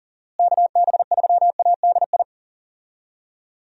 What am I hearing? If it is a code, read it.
Morse code, heard as a keyed tone, about 37 wpm, 700 Hz. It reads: KB3ADI
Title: USA Call Signs - 6 Characters - Repeated 2x - 40wpm